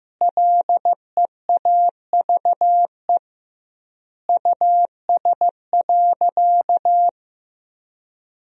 Morse code, heard as a keyed tone, about 15 wpm, 700 Hz. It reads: LEAVE US.
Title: 2x Word-Spacing - Sentences from Top 400 Words - 15wpm